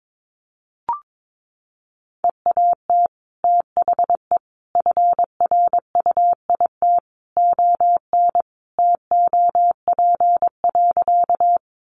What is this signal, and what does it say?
Morse code, heard as a keyed tone, about 22 wpm, 700 Hz. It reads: EAT THE FRUIT ON TOP.
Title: Sentences from Top 1300 Words 22wpm